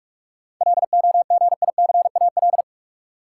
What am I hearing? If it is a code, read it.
Morse code, heard as a keyed tone, about 46 wpm, 700 Hz. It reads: LOGICAL
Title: Seven Letter Words - 45wpm